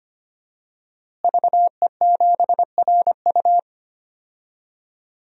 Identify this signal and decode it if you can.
Morse code, heard as a keyed tone, about 25 wpm, 700 Hz. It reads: VE7RU
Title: DXCC-100 Call Signs - 5 Characters - Repeated 2x - 25wpm